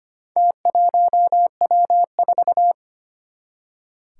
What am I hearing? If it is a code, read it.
Morse code, heard as a keyed tone, about 25 wpm, 700 Hz. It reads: T1W4
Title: Four Challenging Letter-Number Combinations 25wpm